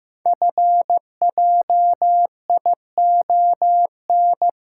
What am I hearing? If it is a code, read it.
Morse code, heard as a keyed tone, about 15 wpm, 700 Hz. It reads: FJION
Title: Code Groups - 5 Letters - 15wpm